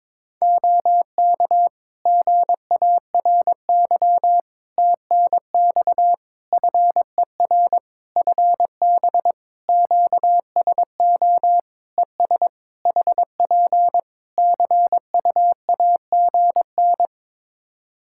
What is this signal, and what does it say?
Morse code, heard as a keyed tone, about 22 wpm, 700 Hz. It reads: OK GARY TNX FER FB QSO ES HP CUAGN
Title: Practice QSO - Exchange 4 - Ending - 22wpm